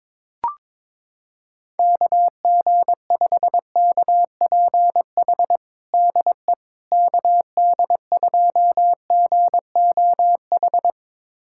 Morse code, heard as a keyed tone, about 22 wpm, 700 Hz. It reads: KG5KPH DE KD2GOH
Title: USA Callsign DE Callsign 22wpm